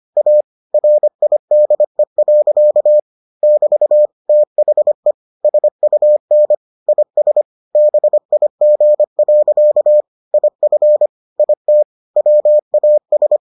Morse code, heard as a keyed tone, about 25 wpm, 600 Hz. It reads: A RIDE. <BT> THE SUN IS BIG. IF IT WAS